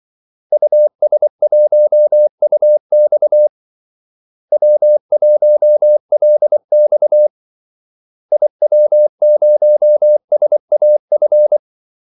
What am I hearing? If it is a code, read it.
Morse code, heard as a keyed tone, about 24 wpm, 600 Hz. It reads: US1UX W1LX IW0SAF
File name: Callsigns-24wpm